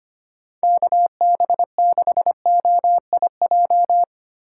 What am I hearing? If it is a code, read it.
Morse code, heard as a keyed tone, about 25 wpm, 700 Hz. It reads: KB6OIJ
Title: USA Call Signs - 6 Characters - 3x Character Spacing - 25wpm